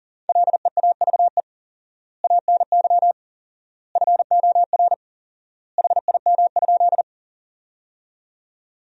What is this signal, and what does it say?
Morse code, heard as a keyed tone, about 40 wpm, 700 Hz. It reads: LEAVE ANY FOR HIM?
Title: 4x Word-Spacing - Sentences from Top 400 Words - 40wpm